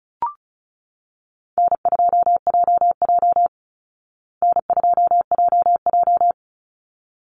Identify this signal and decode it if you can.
Morse code, heard as a keyed tone, about 34 wpm, 700 Hz. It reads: N2JJ N2JJ
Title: Call Signs Repeated 3x - 35wpm